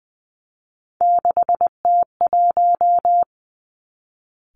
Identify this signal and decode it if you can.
Morse code, heard as a keyed tone, about 20 wpm, 700 Hz. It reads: 6T1
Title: Three Challenging Letter-Number Combinations 20wpm